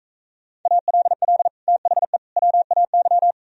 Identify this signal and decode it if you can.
Morse code, heard as a keyed tone, about 44 wpm, 700 Hz. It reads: ALL THE WAY